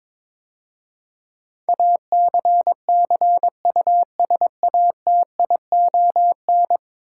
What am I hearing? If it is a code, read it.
Morse code, heard as a keyed tone, about 22 wpm, 700 Hz. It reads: ACCUSATION
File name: practice / 22wpm / 10 Letter Words - First 3 Letters Given - 22wpm